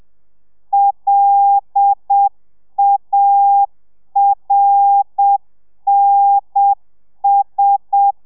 Morse code, heard as a keyed tone, about 7 wpm, 800 Hz. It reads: LARNS